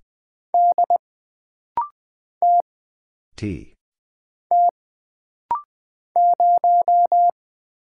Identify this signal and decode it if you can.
Morse code, heard as a keyed tone, about 20 wpm, 700 Hz. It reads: D T T 0
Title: Single Letter-Number Alt - Rapid-Fire 20wpm